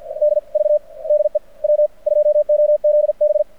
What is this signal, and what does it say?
Morse code, heard as a keyed tone, about 45 wpm, spaced as for 38 wpm, 600 Hz. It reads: DV DE W2OZB
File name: archived / CRSnet